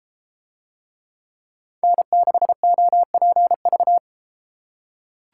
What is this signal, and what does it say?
Morse code, heard as a keyed tone, about 31 wpm, 700 Hz. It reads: N6OPV